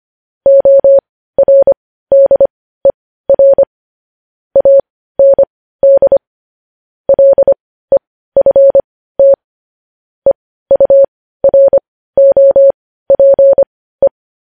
Morse code, heard as a keyed tone, about 25 wpm, spaced as for 15 wpm, 550 Hz. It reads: ORDER AND LEFT EUROPE